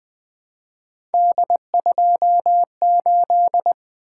Morse code, heard as a keyed tone, about 20 wpm, 700 Hz. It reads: D28
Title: Three Letter-Number Combinations - 20wpm